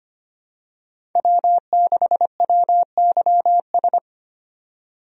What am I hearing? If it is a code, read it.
Morse code, heard as a keyed tone, about 25 wpm, 700 Hz. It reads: W6WYS
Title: USA Call Signs - 5 Characters - 25wpm